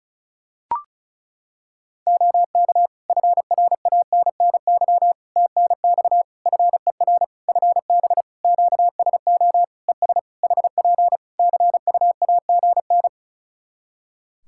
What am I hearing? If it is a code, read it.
Morse code, heard as a keyed tone, about 34 wpm, 700 Hz. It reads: OK FRANNY TNX FER FB QSO ES HP CUAGN
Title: Practice QSO - Exchange 4 - Ending - 35wpm